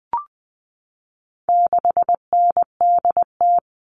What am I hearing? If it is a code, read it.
Morse code, heard as a keyed tone, about 20 wpm, 700 Hz. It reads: 6NDT
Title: All Cut Numbers - 4-Digit 20wpm